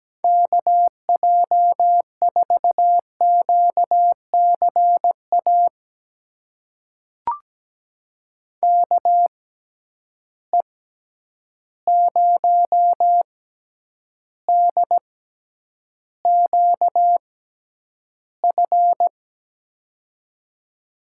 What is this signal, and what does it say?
Morse code, heard as a keyed tone, about 17 wpm, spaced as for 9 wpm, 700 Hz. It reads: KJ4QCA KE0DQF
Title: USA Call Signs - 6 Characters - 6x Character Spacing - 17wpm